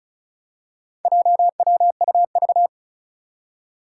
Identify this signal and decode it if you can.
Morse code, heard as a keyed tone, about 34 wpm, 700 Hz. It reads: JWUV
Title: Four Challenging Letter-Number Combinations 35wpm